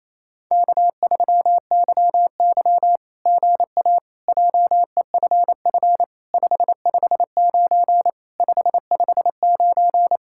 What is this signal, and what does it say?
Morse code, heard as a keyed tone, about 28 wpm, 700 Hz. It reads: K3YY GA JEFF 559 559
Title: SOTA QSO Practice 28wpm